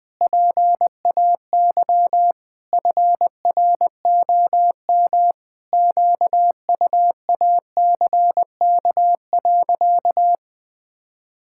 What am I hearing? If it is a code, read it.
Morse code, heard as a keyed tone, about 20 wpm, 700 Hz. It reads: PAY FROM QUACK.
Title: Pangram Challenge v2 - 20wpm